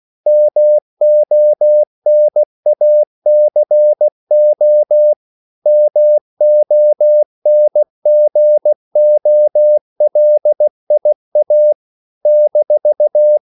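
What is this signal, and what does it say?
Morse code, heard as a keyed tone, about 16 wpm, 600 Hz. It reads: MONACO MONGOLIA -
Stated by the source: CW practice of national capitals at 16 words per minute, from ON6ZQ.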